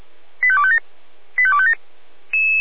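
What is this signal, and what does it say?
Five-tone selective call (ZVEI), heard as quick runs of five tones, tones of about 80 ms. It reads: 85257 85257
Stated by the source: ZVEI five-tone selective call received on an SDR.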